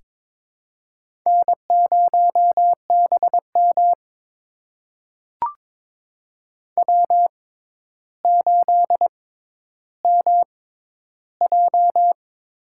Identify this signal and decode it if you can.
Morse code, heard as a keyed tone, about 22 wpm, spaced as for 11 wpm, 700 Hz. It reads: N0BM W8MJ
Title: USA Call Signs - 6x Character Spacing - 22wpm